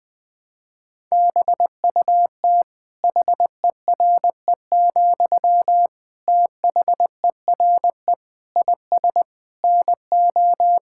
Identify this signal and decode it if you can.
Morse code, heard as a keyed tone, about 20 wpm, 700 Hz. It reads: BUT HERE, THERE IS NO
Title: Sentences from Top 1000 Words - Review Part 6 - 20wpm